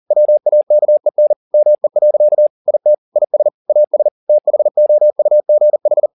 Morse code, heard as a keyed tone, about 40 wpm, 600 Hz. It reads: WAKEN ME. IT IS AS THOUGH